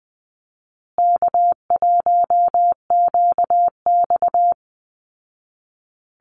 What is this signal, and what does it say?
Morse code, heard as a keyed tone, about 20 wpm, 700 Hz. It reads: K1QX
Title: Easy Call Signs Repeated 3x - 20wpm